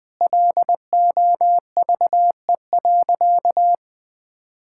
Morse code, heard as a keyed tone, about 21 wpm, 700 Hz. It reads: LOVE.
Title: Sentences from Top 1000 Words - Review Part 5 - 20wpm